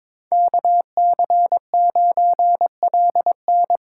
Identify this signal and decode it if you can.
Morse code, heard as a keyed tone, about 22 wpm, 700 Hz. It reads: KC9LN
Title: USA Call Signs - 5 Characters - 3x Character Spacing - 22wpm